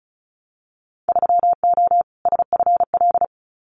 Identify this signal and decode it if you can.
Morse code, heard as a keyed tone, about 34 wpm, 700 Hz. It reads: 3O SFL
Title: Winter Field Day Exchanges 35wpm